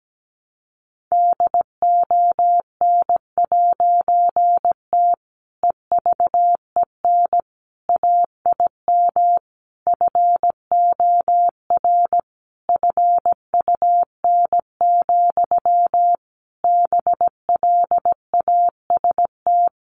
Morse code, heard as a keyed tone, about 17 wpm, 700 Hz. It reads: DON'T EVEN AIM FOR FUN, BLAST
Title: A Sound Of Thunder 17wpm